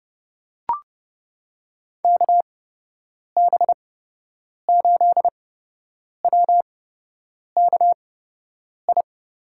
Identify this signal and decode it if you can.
Morse code, heard as a keyed tone, about 30 wpm, spaced as for 8 wpm, 700 Hz. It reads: KB8WKI
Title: USA Call Signs - 6 Characters - 8x Character Spacing - 30wpm